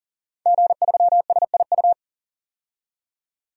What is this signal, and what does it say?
Morse code, heard as a keyed tone, about 40 wpm, 700 Hz. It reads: G3SIU
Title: DXCC-100 Call Signs - 5 Characters - 40wpm